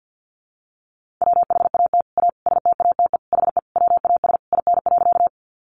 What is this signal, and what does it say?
Morse code, heard as a keyed tone, about 50 wpm, 700 Hz. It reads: WHAT A STATE HE WAS IN.
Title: Sentences from Top 400 Words 50wpm